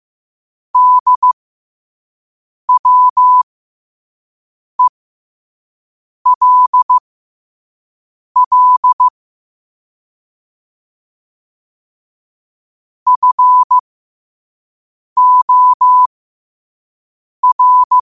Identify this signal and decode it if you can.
Morse code, heard as a keyed tone, about 15 wpm, spaced as for 5 wpm, 1000 Hz. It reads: DWELL FOR
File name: qotdfiles / 5wpm